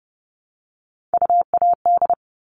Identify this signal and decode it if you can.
Morse code, heard as a keyed tone, about 30 wpm, 700 Hz. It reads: UAD
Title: All Cut Numbers - 3-Digit 30wpm